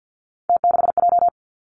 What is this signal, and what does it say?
Morse code, heard as a keyed tone, about 50 wpm, 700 Hz. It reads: T6J
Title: Three Challenging Letter-Number Combinations 50wpm